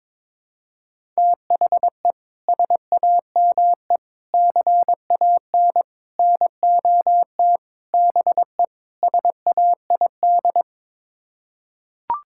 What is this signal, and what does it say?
Morse code, heard as a keyed tone, about 21 wpm, 700 Hz. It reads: THE SAME CAN NOT BE SAID
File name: practice / 22wpm / Sets of 6 Words - 22wpm